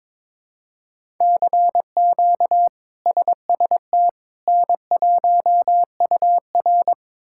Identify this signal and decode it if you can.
Morse code, heard as a keyed tone, about 22 wpm, 700 Hz. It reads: CQ SST N1UR
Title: K1USN SST QSO Practice - 22wpm